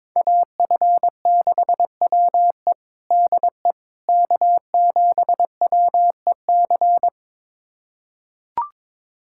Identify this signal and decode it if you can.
Morse code, heard as a keyed tone, about 22 wpm, 700 Hz. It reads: AF6WE DE K7WEC
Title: USA Callsign DE Callsign - 5 Characters - 22wpm